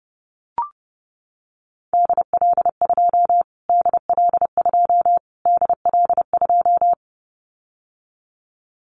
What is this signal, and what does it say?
Morse code, heard as a keyed tone, about 30 wpm, 700 Hz. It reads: DL2 DL2 DL2
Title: Callsign Prefixes for Top 25 DXCC - Repeated 3x - 30wpm